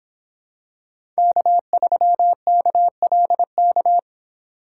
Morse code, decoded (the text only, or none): K3KLK